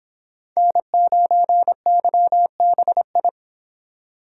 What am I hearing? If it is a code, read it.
Morse code, heard as a keyed tone, about 25 wpm, 700 Hz. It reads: N9YBI